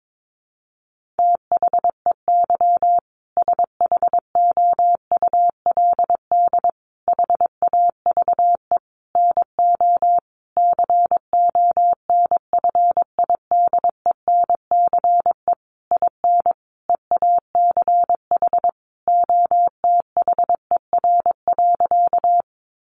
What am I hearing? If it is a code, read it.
Morse code, heard as a keyed tone, about 22 wpm, 700 Hz. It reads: THEY SHOULD HAVE NO CONFIDENCE IN EACH OTHER.
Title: Sentences from Top 1500 Words 22wpm